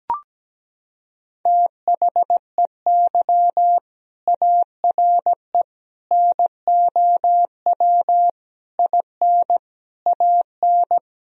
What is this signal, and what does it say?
Morse code, heard as a keyed tone, about 17 wpm, 700 Hz. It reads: THEY ARE NOW IN AN
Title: Sentences from Top 1000 Words - Review Part 2 - 17wpm